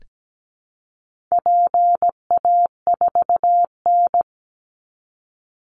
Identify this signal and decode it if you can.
Morse code, heard as a keyed tone, about 17 wpm, 700 Hz. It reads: PA4N
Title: Call Signs Repeated 2x - 17wpm